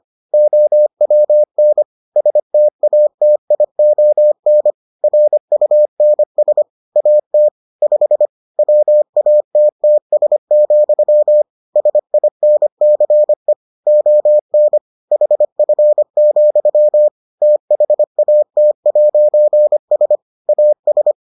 Morse code, heard as a keyed tone, about 25 wpm, 600 Hz. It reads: OWN STATION RUNS AT 5 WATTS, SINCE ON HF, THAT'S AS